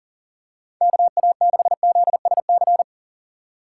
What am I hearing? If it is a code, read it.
Morse code, heard as a keyed tone, about 38 wpm, 700 Hz. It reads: KA6ZSC
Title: USA Call Signs - 6 Characters - 2x Character Spacing - 40wpm